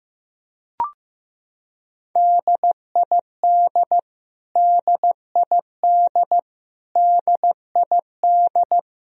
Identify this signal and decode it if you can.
Morse code, heard as a keyed tone, about 15 wpm, 700 Hz. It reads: DID DID DID
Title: Top 100 Words Repeated 15wpm